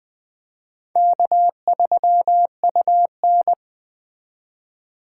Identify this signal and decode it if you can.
Morse code, heard as a keyed tone, about 20 wpm, 700 Hz. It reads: K3UN